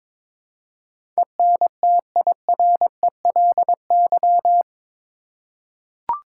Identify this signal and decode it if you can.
Morse code, heard as a keyed tone, about 21 wpm, 700 Hz. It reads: ENTIRELY
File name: practice / 22wpm / Eight Letter Words - 22wpm